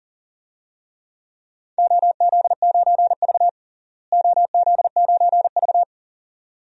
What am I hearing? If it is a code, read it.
Morse code, heard as a keyed tone, about 37 wpm, 700 Hz. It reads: OZ9V OZ9V
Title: DXCC-100 Call Signs - 4 Characters - Repeated 2x - 40wpm